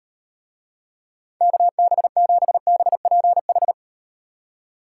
Morse code, heard as a keyed tone, about 38 wpm, 700 Hz. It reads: KB7BPH